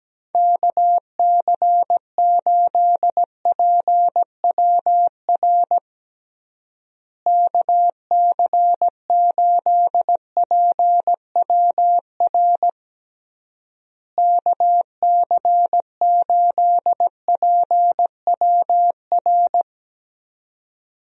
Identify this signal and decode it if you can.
Morse code, heard as a keyed tone, about 17 wpm, 700 Hz. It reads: KC8PWR KC8PWR KC8PWR
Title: USA Call Signs - 6 Characters - Repeated 3x - 17wpm